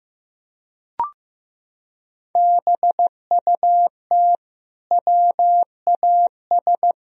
Morse code, heard as a keyed tone, about 15 wpm, 700 Hz. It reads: BUT WAS
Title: Sentences from Top 1000 Words - Review Part 2 - 15wpm